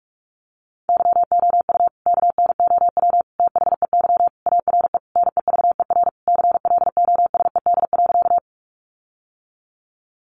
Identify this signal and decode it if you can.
Morse code, heard as a keyed tone, about 44 wpm, 700 Hz. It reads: YOU KNOW THEY ARE NEVER CLOSED.
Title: Sentences from Top 1400 Words 45wpm